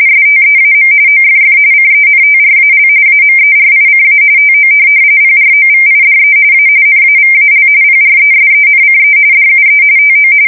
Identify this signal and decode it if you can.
RTTY, about 45.45 baud. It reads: STEAM OFF CENTRAL AMERICA TO PROVIDE A NAVAL PRESENCE DUE TO
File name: fsk170 / USS-New-Jersey